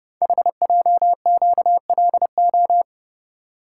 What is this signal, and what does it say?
Morse code, heard as a keyed tone, about 30 wpm, 700 Hz. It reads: HJQLO
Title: Code Groups - 5 Letters - 30wpm